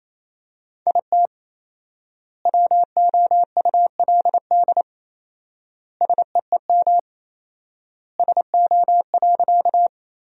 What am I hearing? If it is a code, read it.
Morse code, heard as a keyed tone, about 28 wpm, 700 Hz. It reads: IT WOULD SEEM SO.
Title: 4x Word-Spacing - Sentences from Top 300 Words - 28wpm